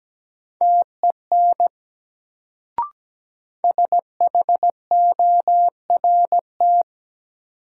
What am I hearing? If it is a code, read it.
Morse code, heard as a keyed tone, about 17 wpm, 700 Hz. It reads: TEN SHORT
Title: Sentences from Top 400 Words - New Word Drill - 17wpm